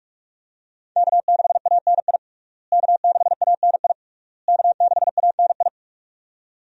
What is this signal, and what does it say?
Morse code, heard as a keyed tone, about 42 wpm, 700 Hz. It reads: K6ANI K6ANI K6ANI
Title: USA Call Signs - 5 Characters - Repeated 3x - 45wpm